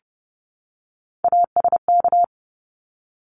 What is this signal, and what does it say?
Morse code, heard as a keyed tone, about 30 wpm, 700 Hz. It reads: ASK
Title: Top 300 Words Repeated 30wpm